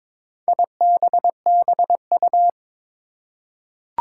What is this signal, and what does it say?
Morse code, heard as a keyed tone, about 22 wpm, 700 Hz. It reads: IBBU
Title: Four Challenging Letter-Number Combinations 22wpm